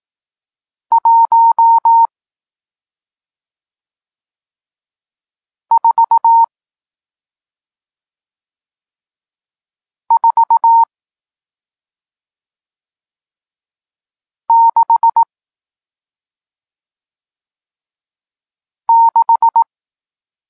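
Morse code, heard as a keyed tone, about 18 wpm, spaced as for 2 wpm, 920 Hz. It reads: 14466
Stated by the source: Morse code lesson teaching digits 1, 4 and 6, from SP9MOA, Niepołomicki Klub Krótkofalowców (Poland) amateur radio club.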